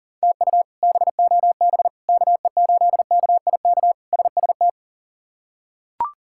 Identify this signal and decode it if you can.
Morse code, heard as a keyed tone, about 40 wpm, 700 Hz. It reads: TU BOB KE8KIK SST
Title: K1USN SST QSO Practice - 40wpm